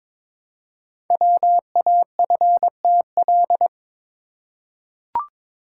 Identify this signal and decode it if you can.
Morse code, heard as a keyed tone, about 22 wpm, 700 Hz. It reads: WAFTL
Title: Code Groups - 5 Letters - 22wpm